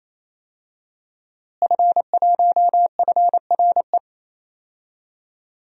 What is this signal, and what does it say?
Morse code, heard as a keyed tone, about 28 wpm, 700 Hz. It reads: F1FRE
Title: DXCC-100 Call Signs - 5 Characters - Repeated 3x - 28wpm